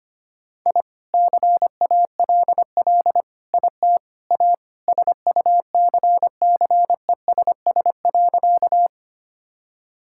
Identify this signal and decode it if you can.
Morse code, heard as a keyed tone, about 25 wpm, 700 Hz. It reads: I CALL IT A SUCCESS.
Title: Sentences from Top 1000 Words - Review Part 6 - 25wpm